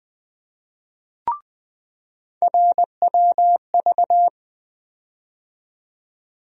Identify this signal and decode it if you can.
Morse code, heard as a keyed tone, about 20 wpm, 700 Hz. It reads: RWV
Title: USA Call Sign Postfixes - 3 Letters 20wpm